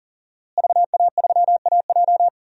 Morse code, heard as a keyed tone, about 40 wpm, 700 Hz. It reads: VA3AJ
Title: DXCC-100 Call Signs - 5 Characters - Repeated 3x - 40wpm